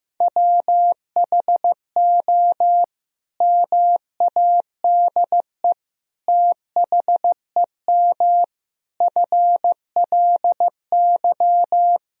Morse code, heard as a keyed tone, about 14 wpm, 700 Hz. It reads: WHO MADE THEM FLY
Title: Sentences from Top 1000 Words - Review Part 2 - 15wpm